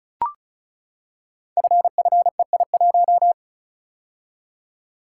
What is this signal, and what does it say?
Morse code, heard as a keyed tone, about 36 wpm, 690 Hz. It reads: FFEI1
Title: Code Groups - 5 Letter:Numbers - 35wpm